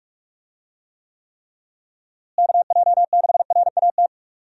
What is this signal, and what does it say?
Morse code, heard as a keyed tone, about 46 wpm, 700 Hz. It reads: KJ6RAT
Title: USA Call Signs - 6 Characters - Repeated 3x - 45wpm